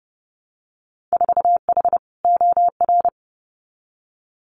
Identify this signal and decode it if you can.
Morse code, heard as a keyed tone, about 30 wpm, 700 Hz. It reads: 4H OR
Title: Winter Field Day Exchanges Encore 30wpm